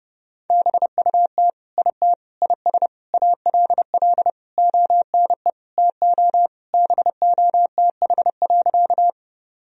Morse code, heard as a keyed tone, about 30 wpm, 700 Hz. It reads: BUT IT IS ALL ONE TO BOTH.
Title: Sentences from Top 1000 Words - Review Part 1 - 30wpm